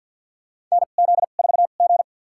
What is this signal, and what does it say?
Morse code, heard as a keyed tone, about 47 wpm, spaced as for 35 wpm, 700 Hz. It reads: NZ4G
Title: USA Call Signs - 4 Characters - 2x Character Spacing - 50wpm